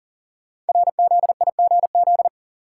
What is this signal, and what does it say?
Morse code, heard as a keyed tone, about 40 wpm, 700 Hz. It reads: RZIGZ
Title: Code Groups - 5 Letters - 40wpm